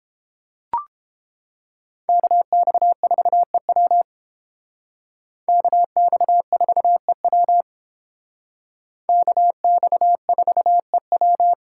Morse code, heard as a keyed tone, about 28 wpm, 700 Hz. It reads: KX4EW KX4EW KX4EW